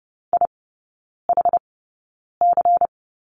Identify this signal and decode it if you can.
Morse code, heard as a keyed tone, about 30 wpm, spaced as for 9 wpm, 700 Hz. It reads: IHC
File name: Lesson 17e - Letter u - Three Characters 30wpm